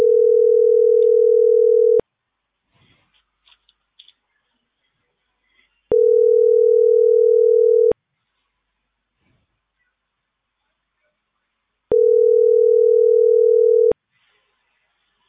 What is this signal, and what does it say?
Ringback tone (North America), 440 + 480 Hz.